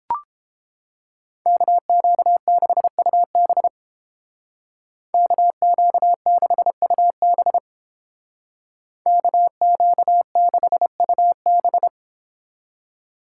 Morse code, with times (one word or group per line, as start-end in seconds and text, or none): 1.5-3.7 KQ6UB
5.1-7.6 KQ6UB
9.1-11.9 KQ6UB